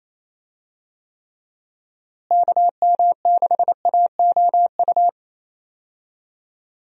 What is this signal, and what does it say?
Morse code, heard as a keyed tone, about 27 wpm, 700 Hz. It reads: KM6AOU
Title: USA Call Signs - 6 Characters - Repeated 3x - 28wpm